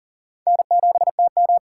Morse code, heard as a keyed tone, about 40 wpm, 700 Hz. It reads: N7TM
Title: USA Call Signs - 4 Characters - 2x Character Spacing - 40wpm